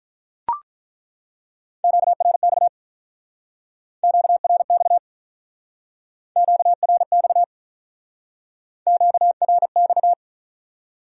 Morse code, heard as a keyed tone, about 47 wpm, 700 Hz. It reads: QRX QRX QRX QRX